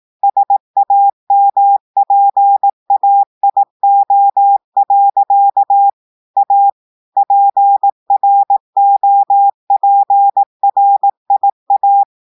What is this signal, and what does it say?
Morse code, heard as a keyed tone, about 18 wpm, 800 Hz. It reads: SAMPAIO. A PROPRIA
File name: audios / portugais / portugais texte en clair 1 vitesse 18 mots minutes